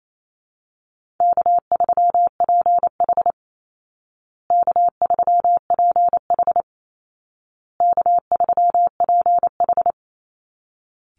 Morse code, heard as a keyed tone, about 28 wpm, 700 Hz. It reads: K3PH K3PH K3PH
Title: Call Signs Repeated 2x - 28wpm